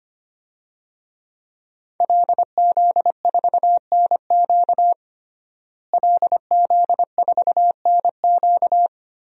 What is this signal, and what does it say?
Morse code, heard as a keyed tone, about 25 wpm, 700 Hz. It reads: LZ4NQ LZ4NQ
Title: DXCC-100 Call Signs - 5 Characters - Repeated 3x - 25wpm